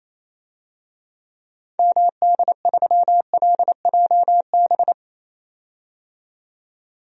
Morse code, heard as a keyed tone, about 28 wpm, 700 Hz. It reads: MD3LJB
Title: DXCC-100 Call Signs - 6 Characters - 28wpm